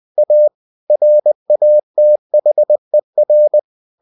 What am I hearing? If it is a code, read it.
Morse code, heard as a keyed tone, about 20 wpm, 600 Hz. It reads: A RATHER
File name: cw-books / winnie-the-pooh / winnie-20wpm-page-0015